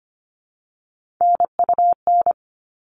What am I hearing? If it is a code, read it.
Morse code, heard as a keyed tone, about 25 wpm, 700 Hz. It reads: NUN
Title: All Cut Numbers - 3-Digit 25wpm